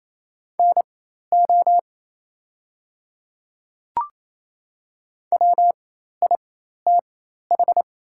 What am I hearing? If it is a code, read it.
Morse code, heard as a keyed tone, about 28 wpm, spaced as for 13 wpm, 700 Hz. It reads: NO WITH
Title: 4x Character Spacing - Top 100 Words - 28wpm